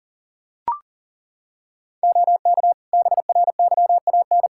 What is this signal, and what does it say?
Morse code, heard as a keyed tone, about 37 wpm, 700 Hz. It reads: OK BRYAN
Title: Practice QSO - Exchange 2 - Info - 40wpm